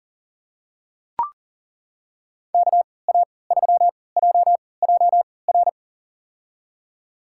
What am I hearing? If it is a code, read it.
Morse code, heard as a keyed tone, about 40 wpm, spaced as for 23 wpm, 700 Hz. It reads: KA3JJR
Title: USA Call Signs - 6 Characters - 3x Character Spacing - 40wpm